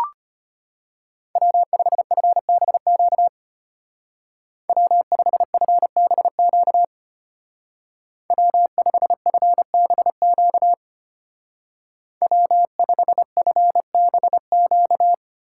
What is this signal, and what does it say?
Morse code, heard as a keyed tone, about 33 wpm, 700 Hz. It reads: W5FBQ W5FBQ W5FBQ W5FBQ